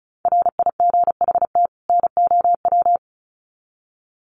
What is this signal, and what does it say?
Morse code, heard as a keyed tone, about 34 wpm, 700 Hz. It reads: RIGHT NOW
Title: Sets of 2 Words 35wpm